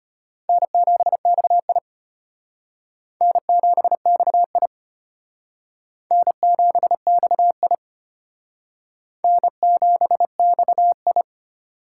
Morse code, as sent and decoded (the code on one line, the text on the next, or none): -. --... -..- .. / -. --... -..- .. / -. --... -..- .. / -. --... -..- ..
N7XI N7XI N7XI N7XI